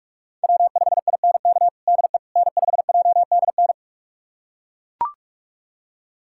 Morse code, heard as a keyed tone, about 42 wpm, 700 Hz. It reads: W5INK DE N5JDN
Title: USA Callsign DE Callsign - 5 Characters - 45wpm